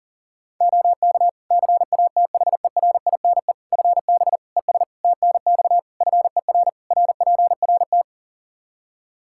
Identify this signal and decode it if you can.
Morse code, heard as a keyed tone, about 40 wpm, 700 Hz. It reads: OK CATHERINE FB ES TNX FER RPRT
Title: Practice QSO - Exchange 2 - Info - 40wpm